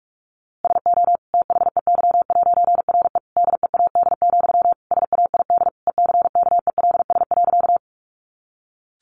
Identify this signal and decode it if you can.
Morse code, heard as a keyed tone, about 44 wpm, 700 Hz. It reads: SO THEY'RE DEAD, SAID ECKELS.
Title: A Sound Of Thunder 45wpm